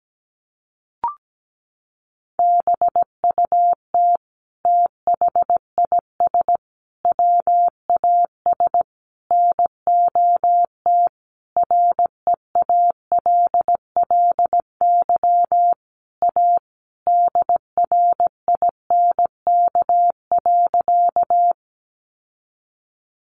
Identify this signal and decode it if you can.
Morse code, heard as a keyed tone, about 17 wpm, 700 Hz. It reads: BUT THIS WAS NOT REALLY A DRINK.
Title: Sentences from Top 1500 Words 17wpm